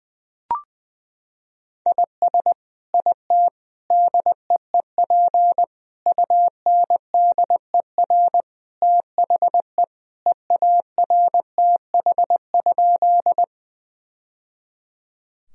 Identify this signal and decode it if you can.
Morse code, heard as a keyed tone, about 21 wpm, 700 Hz. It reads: IS IT DEEP UNDER THE EARTH?
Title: Sentences from Top 1000 Words - Review Part 3 - 20wpm